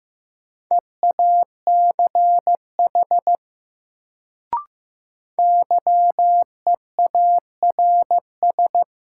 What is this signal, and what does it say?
Morse code, heard as a keyed tone, about 15 wpm, 700 Hz. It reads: EACH YEARS
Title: Sentences from Top 200 Words - New Word Drill - 15wpm